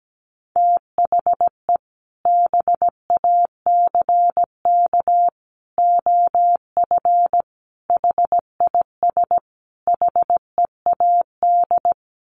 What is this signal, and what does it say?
Morse code, heard as a keyed tone, about 17 wpm, 700 Hz. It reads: THE BACK OF HIS HEAD
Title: Sets of 5 Words Encore 17wpm